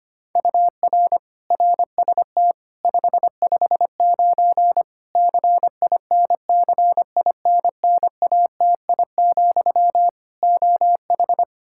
Morse code, heard as a keyed tone, about 25 wpm, 700 Hz. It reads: UR RST 559 CINCINNATI, OH
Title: Practice QSO - Exchange 1 - Beginning - 25wpm